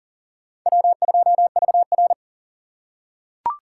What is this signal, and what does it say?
Morse code, heard as a keyed tone, about 40 wpm, 700 Hz. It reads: W2VR
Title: USA Call Signs - 4x Character Spacing - 40wpm